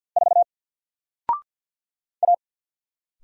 Morse code, heard as a keyed tone, about 50 wpm, 700 Hz. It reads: 4 A
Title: Single Letter-Number Alt 50wpm